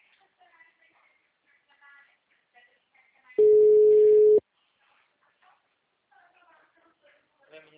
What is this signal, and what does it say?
Ringback tone (Europe, 425 Hz), with speech.
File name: out- -2012- - - .7234